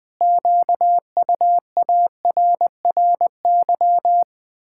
Morse code, heard as a keyed tone, about 21 wpm, 700 Hz. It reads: QUARRY
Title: Pangram Challenge v2 - 20wpm